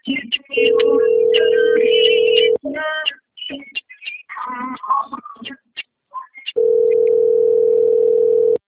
Ringback tone (North America), 440 + 480 Hz.